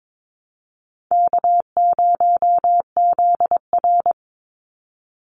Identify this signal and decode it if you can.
Morse code, heard as a keyed tone, about 22 wpm, 700 Hz. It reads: K0ZR
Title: Easy Call Signs - 22wpm